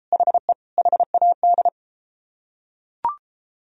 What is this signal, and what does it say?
Morse code, heard as a keyed tone, about 30 wpm, 700 Hz. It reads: HE HAD